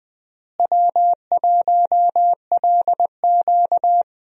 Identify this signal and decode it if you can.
Morse code, heard as a keyed tone, about 20 wpm, 700 Hz. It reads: W1LQ
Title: USA Call Signs - 4 Characters - Repeated 3x - 20wpm